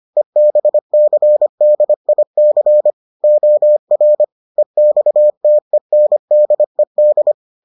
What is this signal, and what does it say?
Morse code, heard as a keyed tone, about 25 wpm, 600 Hz. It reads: EBCDIC OR EXTENDED